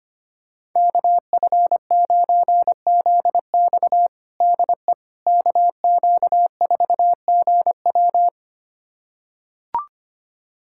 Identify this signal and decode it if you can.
Morse code, heard as a keyed tone, about 25 wpm, 700 Hz. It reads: KF9ZX DE KQ4GW
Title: USA Callsign DE Callsign - 5 Characters - 25wpm